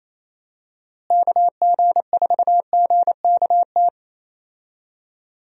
Morse code, heard as a keyed tone, about 28 wpm, 700 Hz. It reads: KG4GKT
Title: USA Call Signs - 6 Characters - Repeated 2x - 28wpm